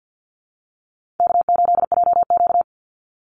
Morse code, heard as a keyed tone, about 50 wpm, 700 Hz. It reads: K8JQ
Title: Easy Call Signs - 50wpm